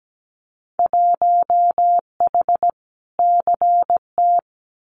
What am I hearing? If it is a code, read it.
Morse code, heard as a keyed tone, about 17 wpm, 700 Hz. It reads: 1H CT
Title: Winter Field Day Exchanges 17wpm